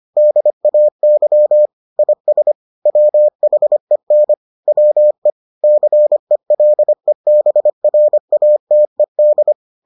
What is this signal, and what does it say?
Morse code, heard as a keyed tone, about 25 wpm, 600 Hz. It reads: DAY IS WHEN WE CELEBRATED